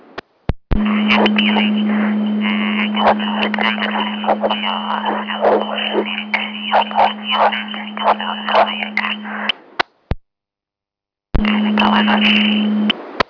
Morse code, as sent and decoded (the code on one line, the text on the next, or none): none